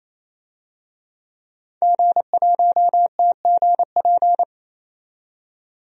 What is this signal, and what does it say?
Morse code, heard as a keyed tone, about 27 wpm, 700 Hz. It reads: G1TGP
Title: DXCC-100 Call Signs - 5 Characters - 28wpm